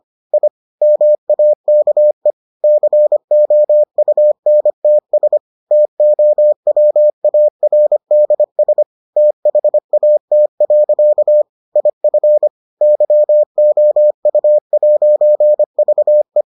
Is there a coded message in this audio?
Yes: Morse code from 0.3 to 16.4 s, heard as a keyed tone, about 25 wpm, 600 Hz. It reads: I MAKE COUNTS TOWARDS THAT. IF YOU'VE